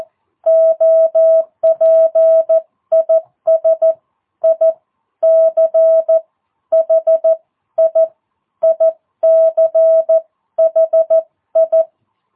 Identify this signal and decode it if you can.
Morse code, heard as a keyed tone, about 14 wpm, spaced as for 12 wpm, 650 Hz. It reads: OP IS ICHI ICHI
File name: jh8cbh / 2022cw